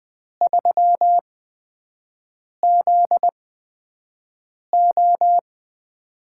Morse code, heard as a keyed tone, about 20 wpm, spaced as for 5 wpm, 700 Hz. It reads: 3ZO
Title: USA Call Signs - 4 Characters - 8x Character Spacing - 20wpm